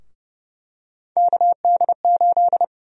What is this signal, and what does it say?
Morse code, heard as a keyed tone, about 30 wpm, 700 Hz. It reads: KD8
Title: USA Call Sign Prefixes - 30wpm